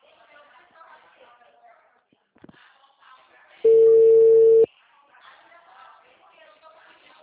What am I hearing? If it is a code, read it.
Ringback tone (China, 450 Hz), with speech.
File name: out- -2012- - - .28177